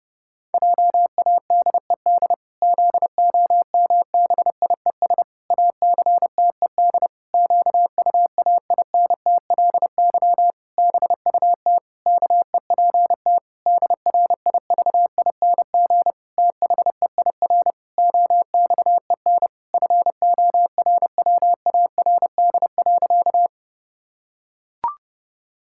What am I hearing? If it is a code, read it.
Morse code, heard as a keyed tone, about 30 wpm, 700 Hz. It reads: JADED ZOMBIES ACTED QUAINTLY BUT KEPT DRIVING THEIR OXEN FORWARD.
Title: Pangram Challenge v2 - 30wpm